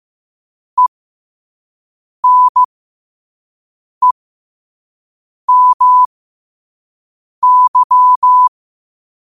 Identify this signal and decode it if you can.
Morse code, heard as a keyed tone, about 15 wpm, spaced as for 5 wpm, 1000 Hz. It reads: ENEMY
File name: qotdfiles / 5wpm